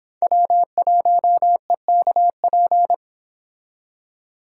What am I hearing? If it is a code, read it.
Morse code, heard as a keyed tone, about 26 wpm, 700 Hz. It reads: W1EKP